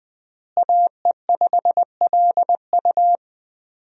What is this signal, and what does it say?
Morse code, heard as a keyed tone, about 21 wpm, 700 Hz. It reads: AE5LU
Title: USA Call Signs - 5 Characters - Repeated 3x - 20wpm